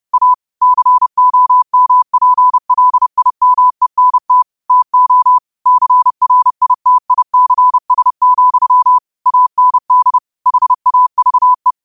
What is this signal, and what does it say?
Morse code, heard as a keyed tone, about 30 wpm, 1000 Hz. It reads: A COMPLIMENT TO CRITICS, AND HAVE
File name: qotdfiles / 30wpm